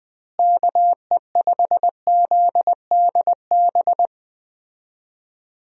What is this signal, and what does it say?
Morse code, heard as a keyed tone, about 21 wpm, 700 Hz. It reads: KE5ZDB
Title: USA Call Signs - 6 Characters - Repeated 2x - 20wpm